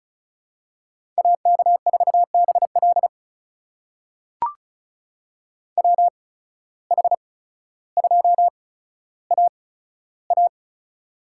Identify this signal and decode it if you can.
Morse code, heard as a keyed tone, about 36 wpm, spaced as for 15 wpm, 700 Hz. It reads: AK4BL WH2AA
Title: USA Call Signs - 5 Characters - 8x Character Spacing - 35wpm